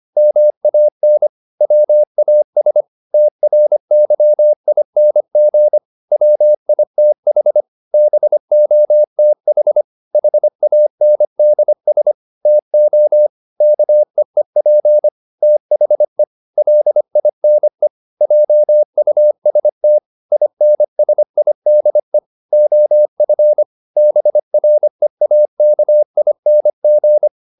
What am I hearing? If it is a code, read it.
Morse code, heard as a keyed tone, about 25 wpm, 600 Hz. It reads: MAN WAS TRYING WITH BOTH HANDS TO KEEP THE LINE JUST INSIDE OF BREAKING
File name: the old man and the sea part 1 25wpm